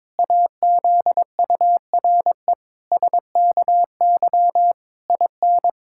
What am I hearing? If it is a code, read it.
Morse code, heard as a keyed tone, about 21 wpm, 700 Hz. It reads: AZURE SKY IN
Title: Pangram Challenge v2 - 22wpm